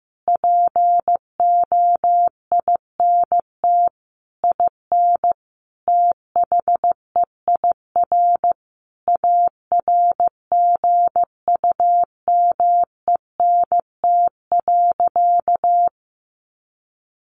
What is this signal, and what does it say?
Morse code, heard as a keyed tone, about 15 wpm, 700 Hz. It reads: POINT IN THEIR ARGUMENT.
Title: Sentences from Top 1500 Words 15wpm